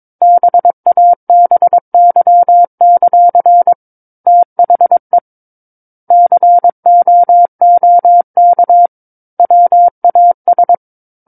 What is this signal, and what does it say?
Morse code, heard as a keyed tone, about 22 wpm, 700 Hz. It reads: BABY; THE COOK WAS
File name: 06-Chapter-06 22 WPM